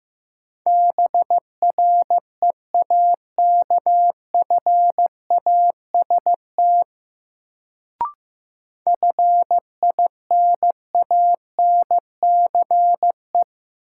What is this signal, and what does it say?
Morse code, heard as a keyed tone, about 15 wpm, 700 Hz. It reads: BREAKFAST FINANCE
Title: Sentences from Top 1500 Words - New Word Drill - 15wpm